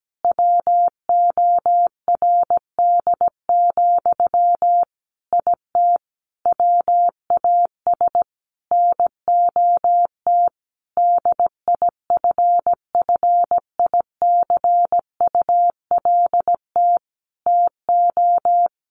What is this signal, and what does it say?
Morse code, heard as a keyed tone, about 17 wpm, 700 Hz. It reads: WORD, IT WAS NOT DIFFICULT TO
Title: Sentences from Top 1100 Words 17wpm